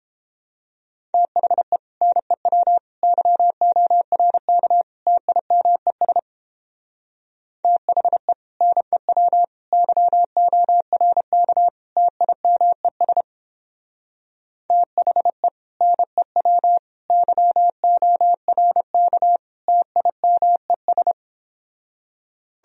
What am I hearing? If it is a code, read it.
Morse code, heard as a keyed tone, about 29 wpm, 700 Hz. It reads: THE NEW YORK TIMES THE NEW YORK TIMES THE NEW YORK TIMES